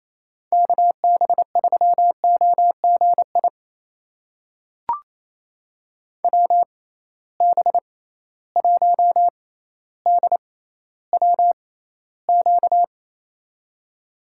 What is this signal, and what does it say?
Morse code, heard as a keyed tone, about 28 wpm, spaced as for 14 wpm, 700 Hz. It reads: KB3OGI WB1DWQ
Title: USA Call Signs - 6 Characters - 6x Character Spacing - 28wpm